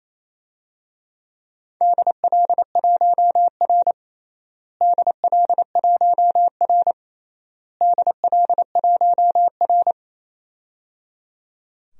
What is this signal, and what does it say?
Morse code, heard as a keyed tone, about 28 wpm, 700 Hz. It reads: DL1R DL1R DL1R
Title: DXCC-100 Call Signs - 4 Characters - Repeated 3x - 28wpm